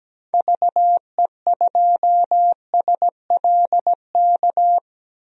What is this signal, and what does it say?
Morse code, heard as a keyed tone, about 17 wpm, 700 Hz. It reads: VE2SLK
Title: DXCC-100 Call Signs - 6 Characters - 17wpm